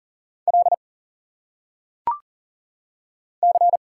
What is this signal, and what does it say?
Morse code, heard as a keyed tone, about 40 wpm, 700 Hz. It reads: L C
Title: Single Letters - Phonetic Pronunciation 40wpm